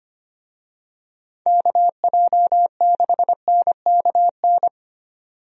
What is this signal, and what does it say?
Morse code, heard as a keyed tone, about 25 wpm, 700 Hz. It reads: KJ6NKN
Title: USA Call Signs - 6 Characters - 25wpm